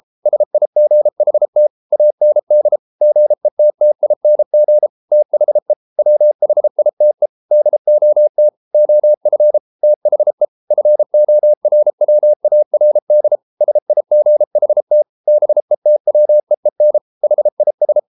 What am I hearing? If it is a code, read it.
Morse code, heard as a keyed tone, about 33 wpm, 600 Hz. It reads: SIGHT AND GETTING THE WHITE DOT OF THE FORWARD SIGHT BETWEEN HIS